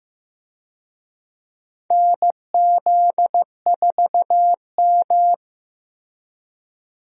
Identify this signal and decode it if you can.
Morse code, heard as a keyed tone, about 15 wpm, 700 Hz. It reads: NZ4M
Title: USA Call Signs - 4 Characters - Repeated 2x - 15wpm